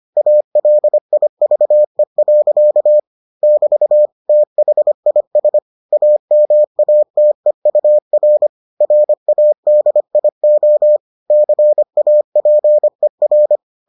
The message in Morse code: ALIVE. <BT> THIS AMATEUR RADIO CAPER

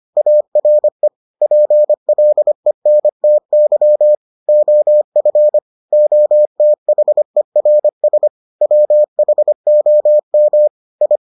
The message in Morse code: ARE PLENTY OF OTHERS WHOM I

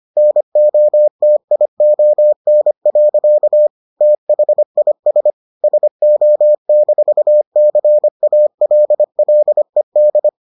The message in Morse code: NOTION. THIS SO-CALLED

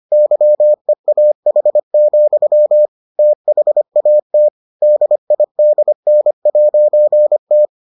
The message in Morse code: YEAH, THAT DIDN'T

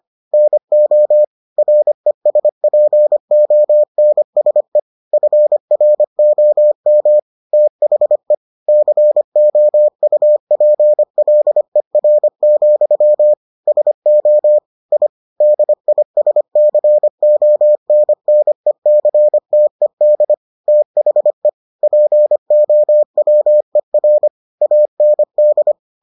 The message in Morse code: NO RESPONSE FROM THE COUPLER, SO I DISCONNECTED THE POWER AND

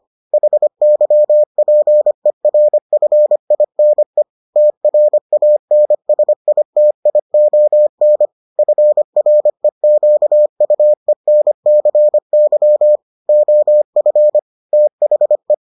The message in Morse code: HYPERFINE TRANSITION FREQUENCY OF THE

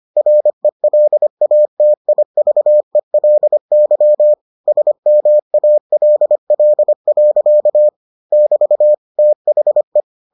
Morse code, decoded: RELATIVELY SMALL. <BT> THE

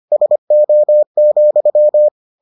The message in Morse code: SO,

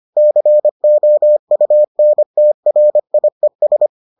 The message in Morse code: COUNTRIES

